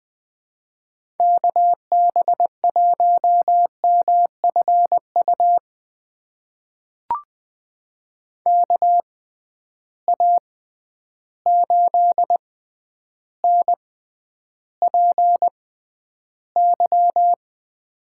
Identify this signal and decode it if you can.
Morse code, heard as a keyed tone, about 20 wpm, spaced as for 10 wpm, 700 Hz. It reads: KB1MFU KA8NPY
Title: USA Call Signs - 6 Characters - 6x Character Spacing - 20wpm